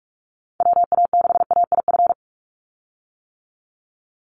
Morse code, heard as a keyed tone, about 44 wpm, 690 Hz. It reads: WA6AIF
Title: USA Call Signs 45wpm